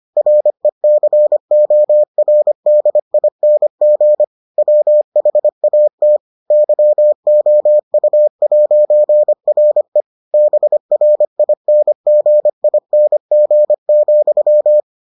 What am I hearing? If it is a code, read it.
Morse code, heard as a keyed tone, about 25 wpm, 600 Hz. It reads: RECORDING WHAT YOU'RE BRINGING,